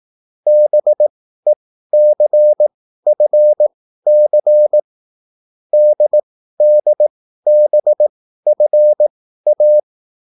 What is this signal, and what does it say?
Morse code, heard as a keyed tone, about 18 wpm, spaced as for 13 wpm, 600 Hz. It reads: BECFC DDBFA